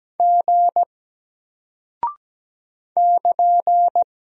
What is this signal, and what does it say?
Morse code, heard as a keyed tone, about 17 wpm, 700 Hz. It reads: G <KN>
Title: Single Character (Letter:Number:Punctuation:Prosign) Alt 17wpm